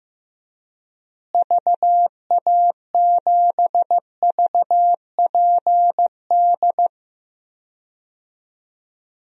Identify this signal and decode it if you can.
Morse code, heard as a keyed tone, about 15 wpm, 700 Hz. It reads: VA7VPD
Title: DXCC-100 Call Signs - 6 Characters - Repeated 3x - 15wpm